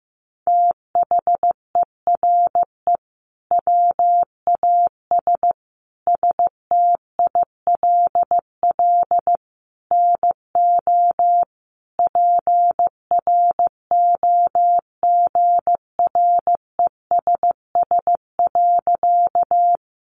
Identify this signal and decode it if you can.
Morse code, heard as a keyed tone, about 14 wpm, 700 Hz. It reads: THERE WAS STILL NO PROGRESS.
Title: Sentences from Top 1400 Words 15wpm